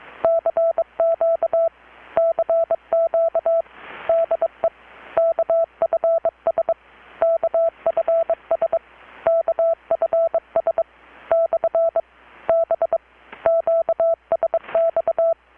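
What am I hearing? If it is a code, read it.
Morse code, heard as a keyed tone, about 23 wpm, 660 Hz. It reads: CQ CQ DE KFS KFS KFS / B QSX
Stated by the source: Morse from KPH's Night of Nights XX maritime radiotelegraphy event, received in Chiapas, Mexico about 3,800 km away on a Kenwood R-600 with a grounded 10 m long-wire antenna.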